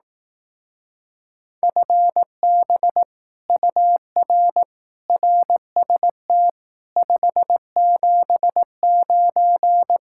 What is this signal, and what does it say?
Morse code, heard as a keyed tone, about 18 wpm, 700 Hz. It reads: FB UR RST 579
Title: Speed-Racing Common QSO Phrases Encore - 15wpm